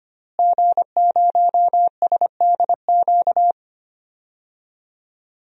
Morse code, heard as a keyed tone, about 25 wpm, 700 Hz. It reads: G0SDQ
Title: DXCC-100 Call Signs - 5 Characters - Repeated 3x - 25wpm